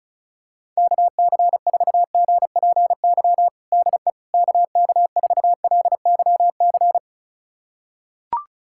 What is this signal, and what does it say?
Morse code, heard as a keyed tone, about 34 wpm, 700 Hz. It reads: KC4GPY DE KK4LYC
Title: USA Callsign DE Callsign - 6 Characters - 35wpm